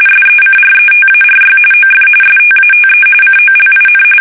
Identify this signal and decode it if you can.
RTTY, about 45.45 baud. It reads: SECRETARY OF STATE COLIN
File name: fsk850 / battleshipnj